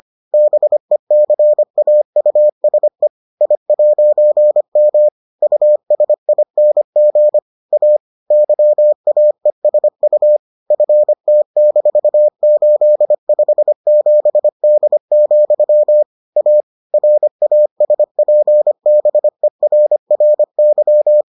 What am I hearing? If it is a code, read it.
Morse code, heard as a keyed tone, about 25 wpm, 600 Hz. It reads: BECAUSE I'M USING A YAESU FT-857D, A RASPBERRY